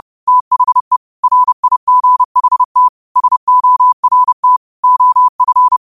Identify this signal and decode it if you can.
Morse code, heard as a keyed tone, about 30 wpm, 1000 Hz. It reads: THE RIGHT SORT OF
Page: New quotes every day in morse code at 30 Words per minute.